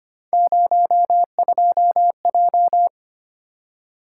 Morse code, heard as a keyed tone, about 25 wpm, 700 Hz. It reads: 02J
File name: Three Letter-Number Combinations - 25wpm